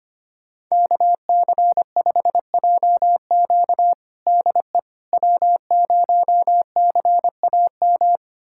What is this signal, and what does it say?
Morse code, heard as a keyed tone, about 25 wpm, 700 Hz. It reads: KC5JQ DE W0CAM
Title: USA Callsign DE Callsign - 5 Characters - 25wpm